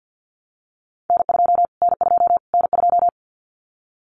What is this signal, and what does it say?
Morse code, heard as a keyed tone, about 50 wpm, 700 Hz. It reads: N2 N2 N2
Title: Callsign Prefixes for Top 25 DXCC - Repeated 3x - 50wpm